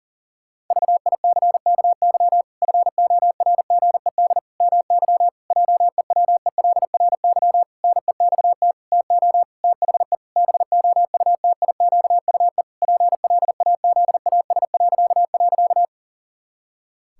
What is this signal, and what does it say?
Morse code, heard as a keyed tone, about 40 wpm, 700 Hz. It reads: VICKY FORGED MY JEWELRY NEXT TO THE BOUTIQUE PLAZAS..
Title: Pangram Challenge v2 - 40wpm